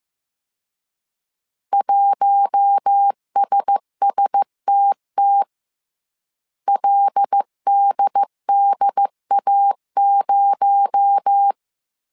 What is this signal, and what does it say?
Morse code, heard as a keyed tone, about 15 wpm, 800 Hz. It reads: 1SSTT LDDA0